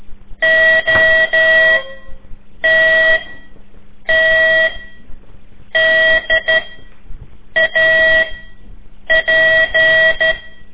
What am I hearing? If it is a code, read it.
Morse code, heard as a keyed tone, about 12 wpm, 1950 Hz, with noise. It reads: O T T D A P
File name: ik1weg2 / CORSI CW / lez4-3